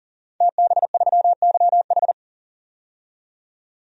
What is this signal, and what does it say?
Morse code, heard as a keyed tone, about 40 wpm, 700 Hz. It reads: TB3YH
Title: DXCC-100 Call Signs - 5 Characters - 40wpm